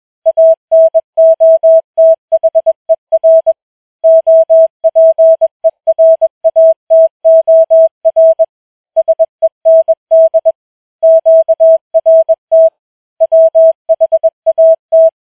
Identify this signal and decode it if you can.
Morse code, heard as a keyed tone, about 21 wpm, 650 Hz. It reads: ANOTHER OPERATOR SEND QRT WHAT
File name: challenge-185